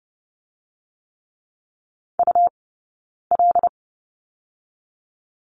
Morse code, heard as a keyed tone, about 30 wpm, 700 Hz. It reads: U L
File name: Lesson 18d - Letter m - Two Characters 30wpm